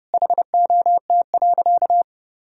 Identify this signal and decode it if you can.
Morse code, heard as a keyed tone, about 30 wpm, 700 Hz. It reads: HOT.
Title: Sentences from Top 1000 Words - Review Part 4 - 30wpm